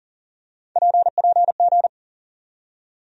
Morse code, heard as a keyed tone, about 40 wpm, 700 Hz. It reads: PPG
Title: USA Call Sign Postfixes - 3 Letters 40wpm